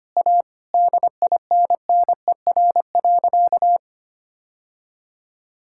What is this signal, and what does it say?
Morse code, heard as a keyed tone, about 25 wpm, 690 Hz. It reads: A DINNER.
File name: Easy Sentences with Hints - 25wpm